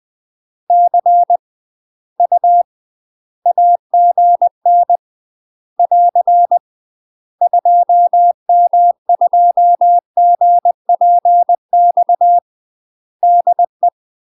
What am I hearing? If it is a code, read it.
Morse code, heard as a keyed tone, about 20 wpm, 700 Hz. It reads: C U AGN <AR> 2M2GPX DE
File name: UK MASTER QSOs TRAINING / 20WPM / UK MASTER QSOs 0017